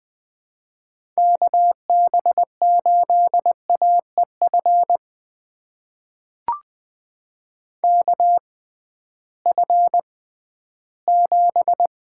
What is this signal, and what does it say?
Morse code, heard as a keyed tone, about 20 wpm, spaced as for 13 wpm, 700 Hz. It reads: KB8AEF KF7
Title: USA Call Signs - 6 Characters - 6x Character Spacing - 20wpm